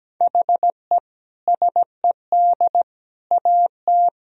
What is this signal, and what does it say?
Morse code, heard as a keyed tone, about 17 wpm, 700 Hz. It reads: HE SED AT
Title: Sentences from Top 300 Words and Abbreviations 17wpm